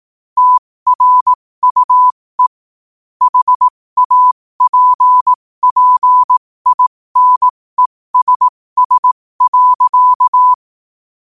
Morse code, heard as a keyed tone, about 18 wpm, spaced as for 15 wpm, 1000 Hz. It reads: TRUE HAPPINESS.